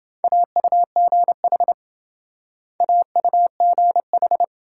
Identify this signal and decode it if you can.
Morse code, heard as a keyed tone, about 25 wpm, 700 Hz. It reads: AUGH AUGH
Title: Speed-Racing Multi-letter Phonemes - 20wpm